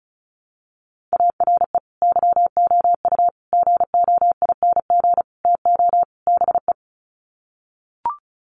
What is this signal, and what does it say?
Morse code, heard as a keyed tone, about 34 wpm, 700 Hz. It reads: ARE YOU GOING TO BE
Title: Sets of 5 Words Encore 35wpm